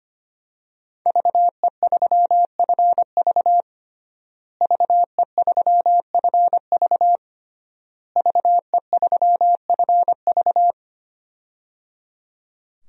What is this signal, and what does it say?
Morse code, heard as a keyed tone, about 25 wpm, 700 Hz. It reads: VE3FV VE3FV VE3FV
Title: DXCC-100 Call Signs - 5 Characters - Repeated 3x - 25wpm